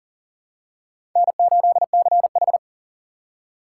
Morse code, heard as a keyed tone, about 38 wpm, 700 Hz. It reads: N8CH
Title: USA Call Signs - 4 Characters - 3x Character Spacing - 40wpm